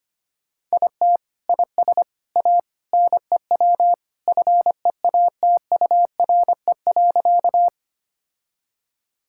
Morse code, heard as a keyed tone, about 25 wpm, 700 Hz. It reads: IT IS A NEW FEATURE.
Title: Sentences from Top 1000 Words - Review Part 5 - 25wpm